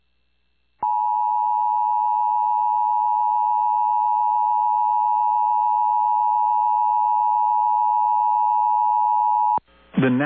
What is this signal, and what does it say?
Emergency Alert System (SAME) tones (the 853 and 960 Hz attention tone). It is heard from 0.8 to 9.6 s.
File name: eas / KUSO / KUSO